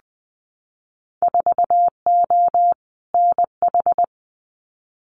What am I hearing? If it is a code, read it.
Morse code, heard as a keyed tone, about 20 wpm, 700 Hz. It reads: 4O NH
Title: Winter Field Day Exchanges Encore 20wpm